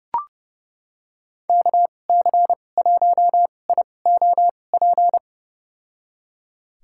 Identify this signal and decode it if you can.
Morse code, heard as a keyed tone, about 30 wpm, spaced as for 22 wpm, 700 Hz. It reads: KC1IOP
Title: USA Call Signs - 6 Characters - 2x Character Spacing - 30wpm